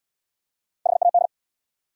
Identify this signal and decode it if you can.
Morse code, heard as a keyed tone, about 68 wpm, 700 Hz. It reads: HAD